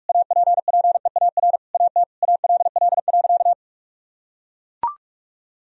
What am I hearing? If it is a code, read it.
Morse code, heard as a keyed tone, about 41 wpm, 700 Hz. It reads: APPEAR AT ALL.
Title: Sentences from Top 1000 Words - Review Part 2 - 45wpm